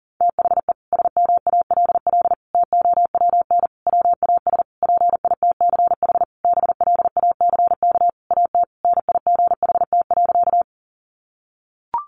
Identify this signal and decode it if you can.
Morse code, heard as a keyed tone, about 40 wpm, 700 Hz. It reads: THE SMALL TOWN WAS PITCH BLACK AT NIGHT.
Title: Sentences from Top 1300 Words 40wpm